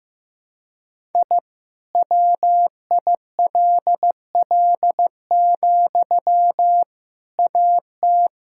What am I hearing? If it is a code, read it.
Morse code, heard as a keyed tone, about 15 wpm, 700 Hz. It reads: I WILL, AT
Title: Sentences from Top 1000 Words - Review Part 5 - 15wpm